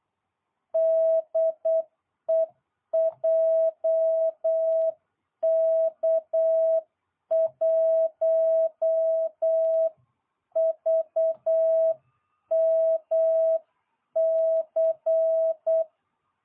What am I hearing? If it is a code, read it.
Morse code, heard as a keyed tone, about 8 wpm, 650 Hz. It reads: DE JK1VMC